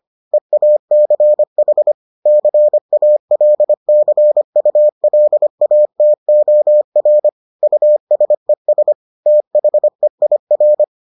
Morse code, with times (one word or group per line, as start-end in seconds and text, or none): 0.3-1.9 EACH
2.3-7.3 CALCULATOR
7.6-8.9 USES
9.3-10.8 THEIR